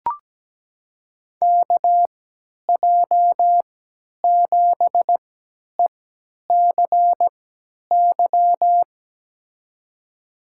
Morse code, heard as a keyed tone, about 17 wpm, spaced as for 10 wpm, 700 Hz. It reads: KJ7ECY
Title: USA Call Signs - 6 Characters - 3x Character Spacing - 17wpm